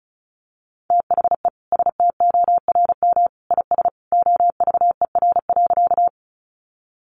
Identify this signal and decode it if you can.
Morse code, heard as a keyed tone, about 34 wpm, 700 Hz. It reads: THE STORM IS OVER.
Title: Sentences from Top 1500 Words 35wpm